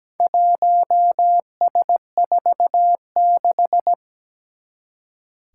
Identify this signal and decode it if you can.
Morse code, heard as a keyed tone, about 17 wpm, 700 Hz. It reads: 1S46
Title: Four Challenging Letter-Number Combinations 17wpm